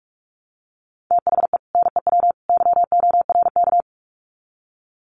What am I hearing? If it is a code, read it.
Morse code, heard as a keyed tone, about 44 wpm, 700 Hz. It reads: THE NEW YORK
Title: Sets of 3 Words 45wpm